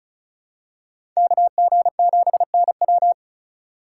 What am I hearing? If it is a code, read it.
Morse code, heard as a keyed tone, about 36 wpm, 700 Hz. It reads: KG7NW
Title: USA Call Signs - 5 Characters - 6x Character Spacing - 35wpm